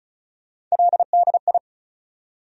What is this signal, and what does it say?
Morse code, heard as a keyed tone, about 34 wpm, 700 Hz. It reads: LDI
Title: USA Call Sign Postfixes - 3 Letters 35wpm